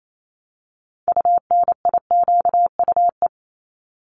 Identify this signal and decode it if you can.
Morse code, heard as a keyed tone, about 28 wpm, 700 Hz. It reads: UNIQUE
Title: Multi-letter Phonemes 28wpm